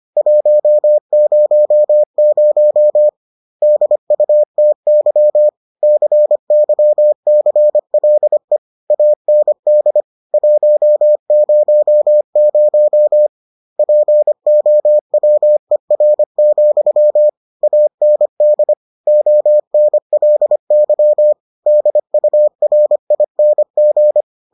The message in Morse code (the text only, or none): 100 DUTY CYCLE AND 100 POWER, AND ONLY DURING